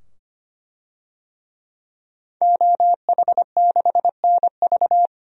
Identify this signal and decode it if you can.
Morse code, heard as a keyed tone, about 25 wpm, 700 Hz. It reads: OH6NV
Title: DXCC-100 Call Signs - 5 Characters - 25wpm